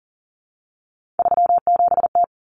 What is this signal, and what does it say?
Morse code, heard as a keyed tone, about 40 wpm, 700 Hz. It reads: 37T
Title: Common Cut Numbers - 3-Digit 40wpm